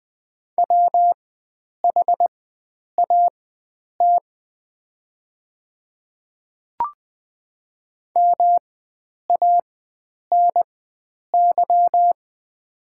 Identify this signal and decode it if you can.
Morse code, heard as a keyed tone, about 20 wpm, spaced as for 9 wpm, 690 Hz. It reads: WHAT MANY
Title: 4x Character Spacing - Top 100 Words - 20wpm